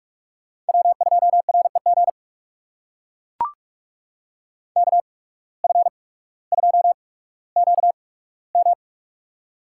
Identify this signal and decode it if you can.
Morse code, heard as a keyed tone, about 44 wpm, spaced as for 19 wpm, 700 Hz. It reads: W2REG KF2QM
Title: USA Call Signs - 5 Characters - 8x Character Spacing - 45wpm